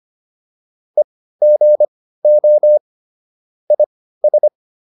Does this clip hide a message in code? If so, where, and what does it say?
Yes: Morse code from 1.0 to 4.5 s, heard as a keyed tone, about 25 wpm, spaced as for 15 wpm, 600 Hz. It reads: EGO IS